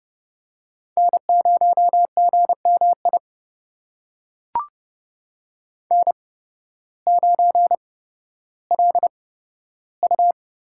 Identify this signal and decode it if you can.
Morse code, heard as a keyed tone, about 30 wpm, spaced as for 14 wpm, 700 Hz. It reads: N0GMI N9LU
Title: USA Call Signs - 5 Characters - 8x Character Spacing - 30wpm